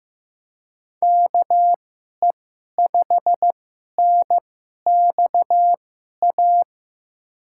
Morse code, heard as a keyed tone, about 15 wpm, spaced as for 11 wpm, 700 Hz. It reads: KE5NXA
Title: USA Call Signs - 6 Characters - 2x Character Spacing - 15wpm